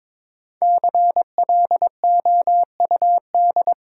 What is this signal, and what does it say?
Morse code, heard as a keyed tone, about 22 wpm, 690 Hz. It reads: CLOUD
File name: 5 Letter Words - First 3 Letters Given - 22wpm